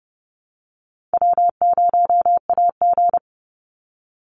Morse code, heard as a keyed tone, about 31 wpm, 700 Hz. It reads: W0AG
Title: Easy Call Signs - 30wpm